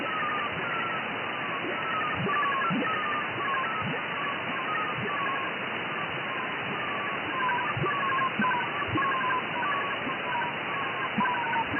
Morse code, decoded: none